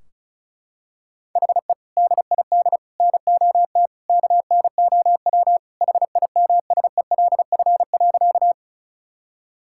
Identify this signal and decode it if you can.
Morse code, heard as a keyed tone, about 36 wpm, 700 Hz. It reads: HE DID NOT KNOW HIMSELF.
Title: Sentences from Top 1600 Words 35wpm